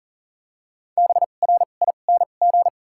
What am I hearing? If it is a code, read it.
Morse code, heard as a keyed tone, about 40 wpm, spaced as for 29 wpm, 700 Hz. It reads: BRING
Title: 2x Character Spacing - Top 500 Words - 40wpm